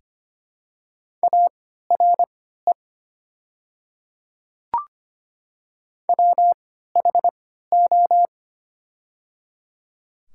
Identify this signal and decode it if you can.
Morse code, heard as a keyed tone, about 25 wpm, spaced as for 14 wpm, 700 Hz. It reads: ARE WHO
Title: 3x Character Spacing - Top 200 Words - 25wpm